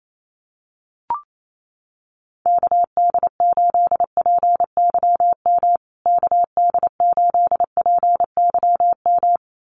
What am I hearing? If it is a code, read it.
Morse code, heard as a keyed tone, about 28 wpm, 700 Hz. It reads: KD8PYM KD8PYM
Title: USA Call Signs Repeated 2x 28wpm